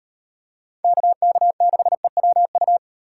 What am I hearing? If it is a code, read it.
Morse code, heard as a keyed tone, about 36 wpm, 700 Hz. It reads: KK6EWU